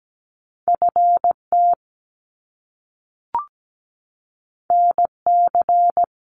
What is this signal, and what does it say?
Morse code, heard as a keyed tone, about 17 wpm, 700 Hz. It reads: FT NC
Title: Top Two Letter Combinations in Words 17wpm